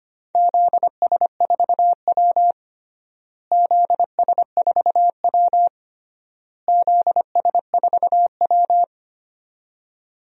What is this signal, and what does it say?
Morse code, heard as a keyed tone, about 25 wpm, 700 Hz. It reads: ZS4W ZS4W ZS4W
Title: DXCC-100 Call Signs - 4 Characters - Repeated 3x - 25wpm